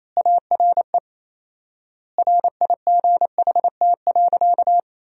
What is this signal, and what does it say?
Morse code, heard as a keyed tone, about 28 wpm, 700 Hz. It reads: ARE RIGHT.
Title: 4x Word-Spacing - Sentences from Top 300 Words - 28wpm